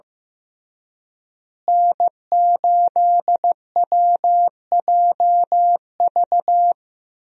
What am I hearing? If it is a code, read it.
Morse code, heard as a keyed tone, about 15 wpm, 700 Hz. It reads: N8WJV
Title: USA Call Signs - 5 Characters - Repeated 3x - 15wpm